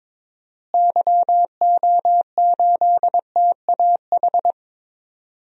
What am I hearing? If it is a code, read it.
Morse code, heard as a keyed tone, about 22 wpm, 700 Hz. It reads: YO8TAH
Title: DXCC-100 Call Signs - 6 Characters - Repeated 2x - 22wpm